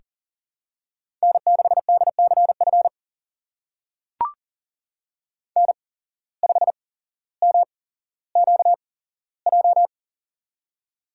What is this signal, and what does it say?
Morse code, heard as a keyed tone, about 40 wpm, spaced as for 17 wpm, 700 Hz. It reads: N6DCF N5MQJ
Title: USA Call Signs - 5 Characters - 8x Character Spacing - 40wpm